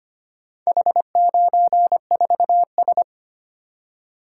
Morse code, heard as a keyed tone, about 25 wpm, 700 Hz. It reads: H94S
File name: practice / 25wpm / Four Letter-Number Combinations - 25wpm